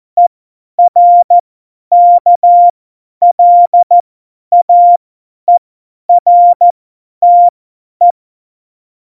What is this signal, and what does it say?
Morse code, heard as a keyed tone, about 14 wpm, spaced as for 10 wpm, 700 Hz. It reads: ERKLAERTE
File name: Morsekurs / Witze / Witz 12. Januar 2021, 50 F70